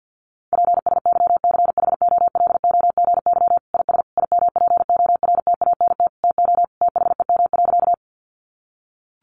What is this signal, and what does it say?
Morse code, heard as a keyed tone, about 50 wpm, 700 Hz. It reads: PSYCHOLOGY IS IMPORTANT TO THEM.
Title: Sentences from Top 1400 Words 50wpm